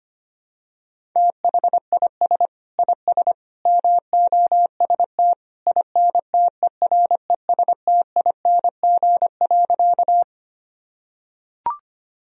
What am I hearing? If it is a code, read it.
Morse code, heard as a keyed tone, about 25 wpm, 700 Hz. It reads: THIS IS MOST INTERESTING.
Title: Sentences from Top 1000 Words - Review Part 1 - 25wpm